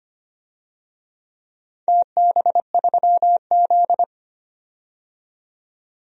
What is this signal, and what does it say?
Morse code, heard as a keyed tone, about 25 wpm, 700 Hz. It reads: TB3Z
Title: DXCC-100 Call Signs - 4 Characters - 25wpm